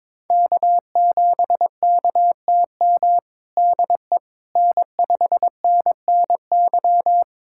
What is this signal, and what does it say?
Morse code, heard as a keyed tone, about 21 wpm, 700 Hz. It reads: K7KTM DE N5NNY
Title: USA Callsign DE Callsign 22wpm